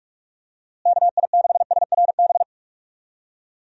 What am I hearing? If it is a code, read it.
Morse code, heard as a keyed tone, about 44 wpm, 700 Hz. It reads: KI6SRB
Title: USA Call Signs - 6 Characters - Repeated 3x - 45wpm